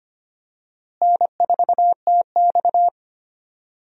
Morse code, heard as a keyed tone, about 25 wpm, 700 Hz. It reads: N4TX
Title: USA Call Signs - 4 Characters - Repeated 3x - 25wpm